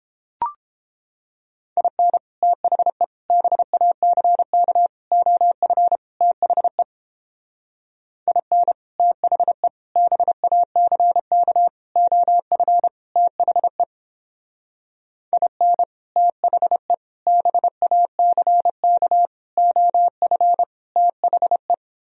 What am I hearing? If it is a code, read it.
Morse code, heard as a keyed tone, about 29 wpm, 700 Hz. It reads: IN THE BACK OF THE IN THE BACK OF THE IN THE BACK OF THE